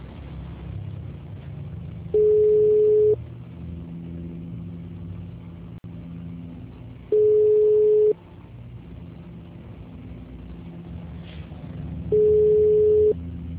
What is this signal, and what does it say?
Ringback tone (Europe, 425 Hz).
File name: out- -2018- - - .29765